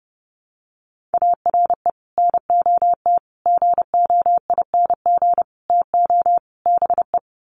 Morse code, heard as a keyed tone, about 30 wpm, 700 Hz. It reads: ARE NOT GOING TO BE
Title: Sets of 5 Words Encore 30wpm